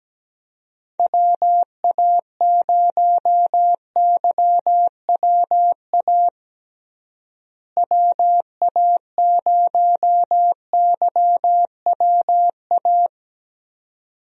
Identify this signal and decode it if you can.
Morse code, heard as a keyed tone, about 17 wpm, 700 Hz. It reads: WA0YWA WA0YWA
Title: USA Call Signs - 6 Characters - Repeated 2x - 17wpm